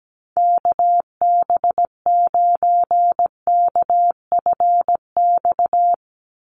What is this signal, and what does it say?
Morse code, heard as a keyed tone, about 17 wpm, 700 Hz. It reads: KB9KFX
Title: USA Call Signs 17wpm